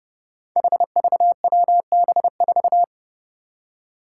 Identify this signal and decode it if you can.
Morse code, heard as a keyed tone, about 31 wpm, 700 Hz. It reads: HVWB4
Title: Five Challenging Letter-Number Combinations 30wpm